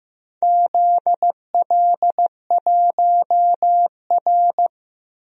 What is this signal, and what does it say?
Morse code, heard as a keyed tone, about 15 wpm, 700 Hz. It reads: ZL1R
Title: DXCC-100 Call Signs - 4 Characters - Repeated 3x - 15wpm